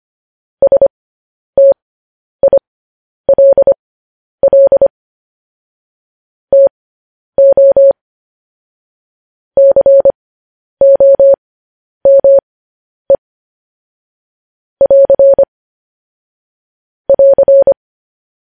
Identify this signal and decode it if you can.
Morse code, heard as a keyed tone, about 25 wpm, spaced as for 10 wpm, 550 Hz. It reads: STILL TO COME <AR> <AR>
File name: podcast / News headlines-1x-25x10